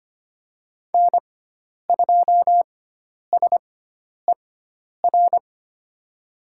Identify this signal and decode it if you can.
Morse code, heard as a keyed tone, about 25 wpm, spaced as for 10 wpm, 700 Hz. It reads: N2SER
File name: Lesson 16g - Letter c - Call Signs 25wpm